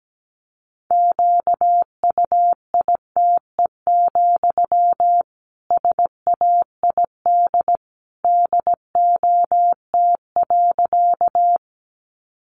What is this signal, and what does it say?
Morse code, heard as a keyed tone, about 17 wpm, 700 Hz. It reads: QUITE, SAID DOT.
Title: Sentences from Top 1500 Words 17wpm